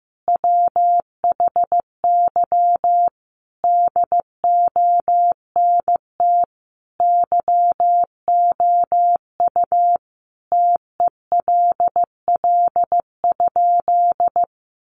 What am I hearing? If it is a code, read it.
Morse code, heard as a keyed tone, about 15 wpm, 700 Hz. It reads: WHY DONT YOU TELL?
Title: Sentences from Top 300 Words 15wpm